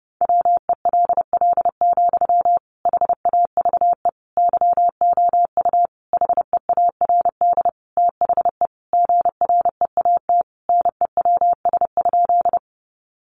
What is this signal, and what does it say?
Morse code, heard as a keyed tone, about 30 wpm, 700 Hz. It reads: WELL, HAVE YOU HEARD THE GREAT NEWS?
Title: Sentences from Top 800 Words 30wpm